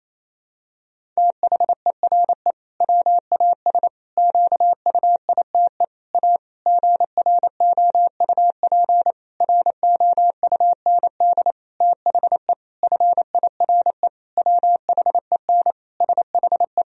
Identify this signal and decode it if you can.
Morse code, heard as a keyed tone, about 28 wpm, 700 Hz. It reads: THERE WAS QUITE A GROUP ROUND THE FIRE WHEN SHE
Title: Sentences from Top 1000 Words - Review Part 1 - 28wpm